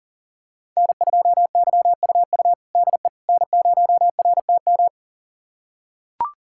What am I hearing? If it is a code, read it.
Morse code, heard as a keyed tone, about 40 wpm, 700 Hz. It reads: N2YUU DE N0RTM
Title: USA Callsign DE Callsign - 5 Characters - 40wpm